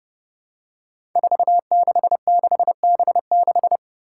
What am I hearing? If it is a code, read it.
Morse code, heard as a keyed tone, about 30 wpm, 700 Hz. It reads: 466B6
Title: Five Challenging Letter-Number Combinations 30wpm